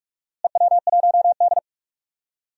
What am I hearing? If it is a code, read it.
Morse code, heard as a keyed tone, about 43 wpm, 700 Hz. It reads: EW1D
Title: DXCC-100 Call Signs - 4 Characters - Repeated 3x - 45wpm